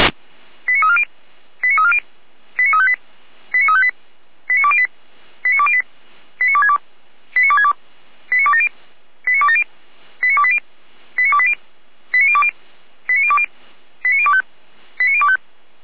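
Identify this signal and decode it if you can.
Five-tone selective call (ZVEI), heard as quick runs of five tones, tones of about 70 ms. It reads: 89340 89340 89358 89358 89208 89208 89262 89262 89270 89270 89280 89280 89020 89020 89025 89025
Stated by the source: Received on an SDR.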